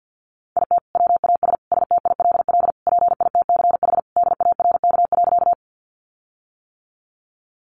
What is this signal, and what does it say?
Morse code, heard as a keyed tone, about 50 wpm, 700 Hz. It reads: IT WAS STILL PITCH DARK.
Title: Sentences from Top 1300 Words 50wpm